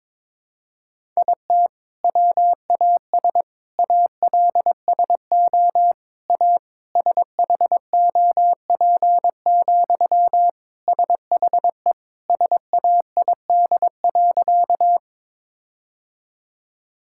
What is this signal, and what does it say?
Morse code, heard as a keyed tone, about 22 wpm, 700 Hz. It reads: IT WAS ALSO A SHOP, SHE SAID.
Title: Sentences from Top 1600 Words 22wpm